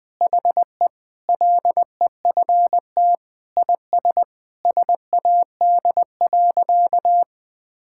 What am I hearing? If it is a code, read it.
Morse code, heard as a keyed tone, about 21 wpm, 700 Hz. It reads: HE LEFT IS SAD.
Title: Sentences with 2-Word Phrases - 20wpm